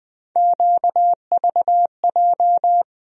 Morse code, heard as a keyed tone, about 20 wpm, 700 Hz. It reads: QVJ
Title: USA Call Sign Postfixes - 3 Letters 20wpm